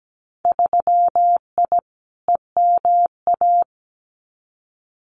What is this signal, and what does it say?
Morse code, heard as a keyed tone, about 17 wpm, 700 Hz. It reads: 3I EMA
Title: Winter Field Day Exchanges Encore 17wpm